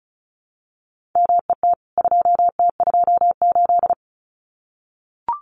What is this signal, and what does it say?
Morse code, heard as a keyed tone, about 34 wpm, 700 Hz. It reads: MET 2T28
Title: CWT Exchange 35wpm